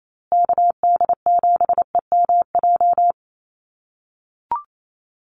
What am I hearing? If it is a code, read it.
Morse code, heard as a keyed tone, about 28 wpm, 700 Hz. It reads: KD7EMJ
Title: USA Call Signs Repeated 3x 28wpm